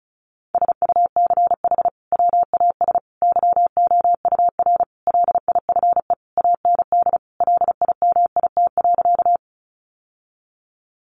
Morse code, heard as a keyed tone, about 34 wpm, 700 Hz. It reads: SUCH WAS YOUR LIFE AND LIMIT.
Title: Sentences from Top 1500 Words 35wpm